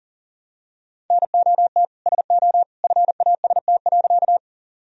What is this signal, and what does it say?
Morse code, heard as a keyed tone, about 40 wpm, 700 Hz. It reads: NOT SO FAST.
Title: Sentences from Top 1000 Words - Review Part 2 - 40wpm